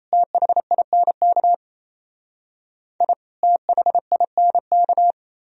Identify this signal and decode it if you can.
Morse code, heard as a keyed tone, about 34 wpm, 700 Hz. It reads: THINK I THINK
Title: Speed-Racing - Sets of 2 Words 28wpm